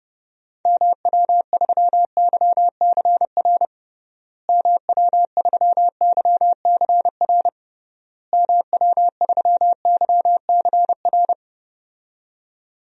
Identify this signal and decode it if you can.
Morse code, heard as a keyed tone, about 30 wpm, 700 Hz. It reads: MW3YCR MW3YCR MW3YCR
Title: DXCC-100 Call Signs - 6 Characters - Repeated 3x - 30wpm